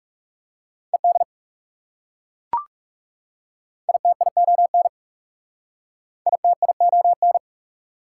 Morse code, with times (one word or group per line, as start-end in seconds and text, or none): 0.9-1.2 ED
3.9-4.9 ITION
6.3-7.4 ITION